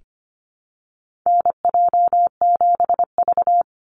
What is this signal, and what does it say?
Morse code, heard as a keyed tone, about 25 wpm, 700 Hz. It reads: NJ7V
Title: NA SOTA Chaser Callsigns 25wpm